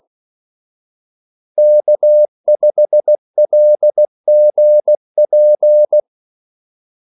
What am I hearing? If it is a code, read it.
Morse code, heard as a keyed tone, about 16 wpm, 600 Hz. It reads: K5LGP
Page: Kaldesignaler 16 wpm